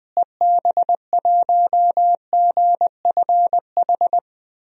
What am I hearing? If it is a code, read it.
Morse code, heard as a keyed tone, about 21 wpm, 700 Hz. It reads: EB1GFH
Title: DXCC-100 Call Signs - 6 Characters - Repeated 3x - 20wpm